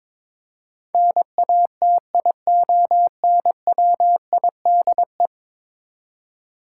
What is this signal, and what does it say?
Morse code, heard as a keyed tone, about 22 wpm, 700 Hz. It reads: NATIONWIDE
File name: practice / 22wpm / 10 Letter Words - First 3 Letters Given - 22wpm